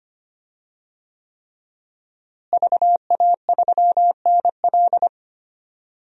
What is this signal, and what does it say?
Morse code, heard as a keyed tone, about 25 wpm, 700 Hz. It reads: VA3NL
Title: DXCC-100 Call Signs - 5 Characters - Repeated 3x - 25wpm